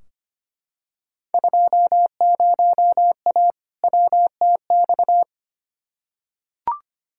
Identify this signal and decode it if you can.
Morse code, heard as a keyed tone, about 25 wpm, 700 Hz. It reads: 20A WTX
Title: Field Day Exchanges Encore - 25wpm